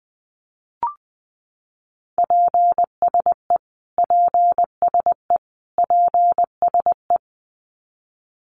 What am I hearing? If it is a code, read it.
Morse code, heard as a keyed tone, about 20 wpm, 700 Hz. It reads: PSE PSE PSE
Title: Abbreviations Repeated 3x - 20wpm